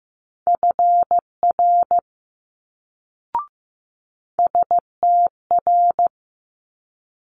Morse code, heard as a keyed tone, about 15 wpm, 700 Hz. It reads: FR STR
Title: Multi-letter Phonemes Quick Fire 15wpm